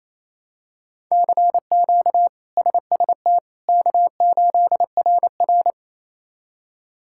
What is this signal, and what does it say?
Morse code, heard as a keyed tone, about 28 wpm, 700 Hz. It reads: CQ SST K8RR
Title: K1USN SST QSO Practice - 28wpm